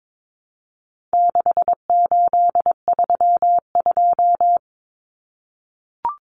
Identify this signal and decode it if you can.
Morse code, heard as a keyed tone, about 22 wpm, 700 Hz. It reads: 6832
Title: Numbers - 4-Digit 22wpm